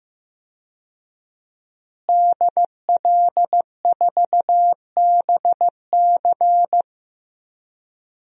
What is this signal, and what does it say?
Morse code, heard as a keyed tone, about 15 wpm, 700 Hz. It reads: DL4BC
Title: DXCC-100 Call Signs - 5 Characters - Repeated 2x - 15wpm